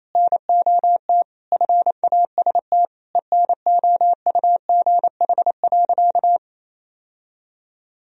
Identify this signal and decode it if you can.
Morse code, heard as a keyed tone, about 27 wpm, 690 Hz. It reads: NOT FAST ENOUGH.
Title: Sentences from Top 1000 Words - Review Part 3 - 28wpm